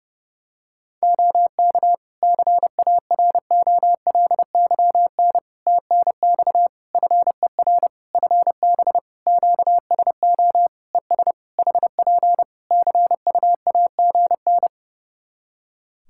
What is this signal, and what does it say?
Morse code, heard as a keyed tone, about 30 wpm, 700 Hz. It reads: OK CAROLYN TNX FER FB QSO ES HP CUAGN
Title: Practice QSO - Exchange 4 - Ending - 30wpm